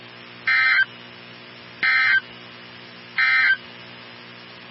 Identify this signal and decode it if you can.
Emergency Alert System (SAME) tones, heard as harsh buzzing bursts. It is heard from 0.4 to 3.6 s.
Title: NOAA Weather Radio 162.550